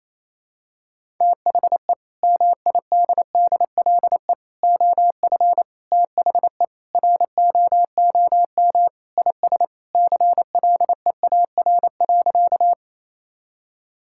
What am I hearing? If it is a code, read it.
Morse code, heard as a keyed tone, about 28 wpm, 700 Hz. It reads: THE MIDDLE OF THE ROOM IS CLEAR.
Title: Sentences from Top 1000 Words - Review Part 6 - 28wpm